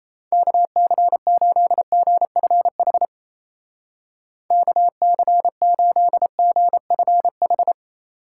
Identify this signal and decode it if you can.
Morse code, heard as a keyed tone, about 37 wpm, 700 Hz. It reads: KC8GFH KC8GFH